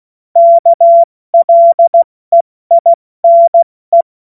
Morse code, heard as a keyed tone, about 16 wpm, 680 Hz. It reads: KLEINE
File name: Morsekurs / Witze / Witz 28.November 2023 70 F80